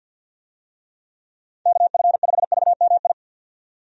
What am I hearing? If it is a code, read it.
Morse code, heard as a keyed tone, about 52 wpm, 700 Hz. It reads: KF5VMI
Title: USA Call Signs - 6 Characters - 50wpm